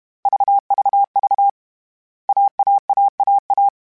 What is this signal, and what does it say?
Morse code, heard as a keyed tone, about 32 wpm, 800 Hz. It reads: VVV AAAAA
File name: CW Code Groups-mp3s / Sets mp3s / AUV AUV4 sp @30